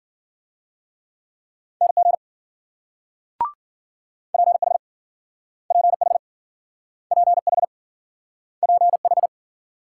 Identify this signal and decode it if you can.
Morse code, heard as a keyed tone, about 61 wpm, 700 Hz. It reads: NG PH PH PH PH